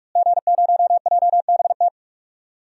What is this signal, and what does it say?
Morse code, heard as a keyed tone, about 41 wpm, 700 Hz. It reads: G0JBT
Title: DXCC-100 Call Signs - 5 Characters - Repeated 3x - 45wpm